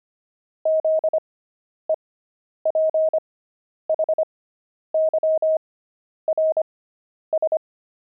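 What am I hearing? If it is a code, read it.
Morse code, heard as a keyed tone, about 25 wpm, spaced as for 10 wpm, 630 Hz. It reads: ZEPHYRS
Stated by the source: Pangram Morse practice at 25 wpm character speed with 10 wpm Farnsworth spacing, intermediate level.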